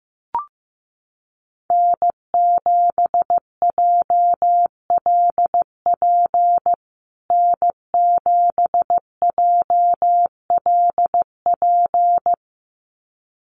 Morse code, heard as a keyed tone, about 15 wpm, 700 Hz. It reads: N7JLP N7JLP
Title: USA Call Signs Repeated 2x 15wpm